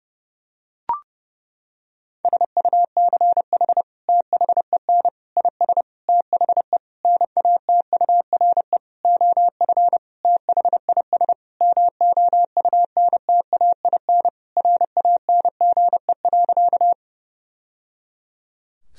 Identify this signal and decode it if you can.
Morse code, heard as a keyed tone, about 30 wpm, 700 Hz. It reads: SUCH THEN IS THE NATURE OF THIS MOUNTAIN RANGE.
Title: Sentences from Top 1000 Words - Review Part 4 - 30wpm